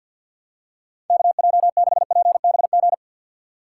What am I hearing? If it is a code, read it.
Morse code, heard as a keyed tone, about 52 wpm, 700 Hz. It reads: KJ6PBG
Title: USA Call Signs - 2x Character Spacing - 50wpm